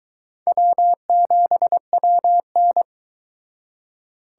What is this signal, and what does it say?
Morse code, heard as a keyed tone, about 24 wpm, 700 Hz. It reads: W7WN